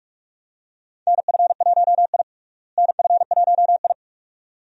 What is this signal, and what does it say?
Morse code, heard as a keyed tone, about 46 wpm, 700 Hz. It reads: NF1I NF1I
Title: USA Call Signs - 4 Characters - Repeated 2x - 45wpm